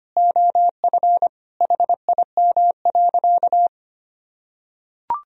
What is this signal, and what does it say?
Morse code, heard as a keyed tone, about 25 wpm, 700 Hz. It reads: OF HIM.
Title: Sentences from Top 1000 Words - Review Part 3 - 25wpm